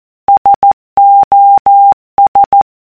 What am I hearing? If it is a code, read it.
Morse code, heard as a keyed tone, about 14 wpm, 800 Hz. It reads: SOS
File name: symbol / morse wav / sos